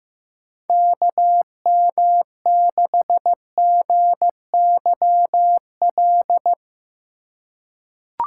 Morse code, heard as a keyed tone, about 15 wpm, 700 Hz. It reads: KM6GYL
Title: USA Call Signs - 4x Character Spacing - 15wpm